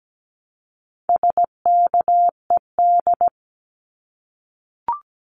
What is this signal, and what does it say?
Morse code, heard as a keyed tone, about 17 wpm, 700 Hz. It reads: SKED
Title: Abbreviations Repeated 3x - 17wpm